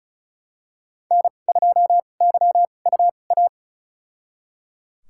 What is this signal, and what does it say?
Morse code, heard as a keyed tone, about 36 wpm, spaced as for 25 wpm, 700 Hz. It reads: N2YUA
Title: USA Call Signs - 5 Characters - 2x Character Spacing - 35wpm